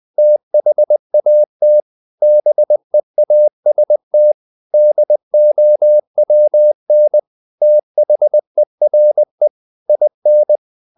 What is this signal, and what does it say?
Morse code, heard as a keyed tone, about 20 wpm, 600 Hz. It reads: THAT BEAST DOWN THERE IN
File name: the old man and the sea / the old man and the sea part 1 20wpm